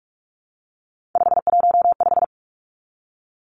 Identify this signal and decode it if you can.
Morse code, heard as a keyed tone, about 46 wpm, 700 Hz. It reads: 515
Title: Three Challenging Letter-Number Combinations 45wpm